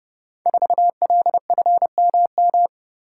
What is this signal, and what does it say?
Morse code, heard as a keyed tone, about 28 wpm, 700 Hz. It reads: 4LFMM
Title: Code Groups - 5 Letter:Numbers - 30wpm